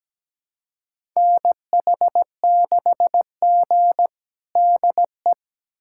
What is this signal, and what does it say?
Morse code, heard as a keyed tone, about 17 wpm, 700 Hz. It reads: NH6G DE
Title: USA Callsign DE Callsign - 4 Characters - 17wpm